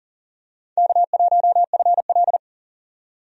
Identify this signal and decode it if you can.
Morse code, heard as a keyed tone, about 40 wpm, 700 Hz. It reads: K1FL
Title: USA Call Signs - 4 Characters - 40wpm